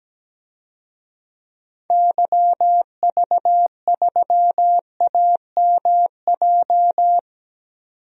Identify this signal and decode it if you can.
Morse code, heard as a keyed tone, about 17 wpm, 700 Hz. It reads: YV3AMJ
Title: DXCC-100 Call Signs - 6 Characters - 17wpm